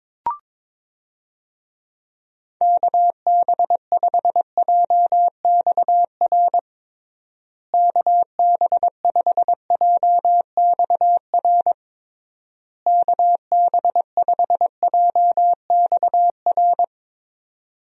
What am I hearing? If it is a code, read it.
Morse code, heard as a keyed tone, about 22 wpm, 700 Hz. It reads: KB5JXR KB5JXR KB5JXR
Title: USA Call Signs - 6 Characters - Repeated 3x - 22wpm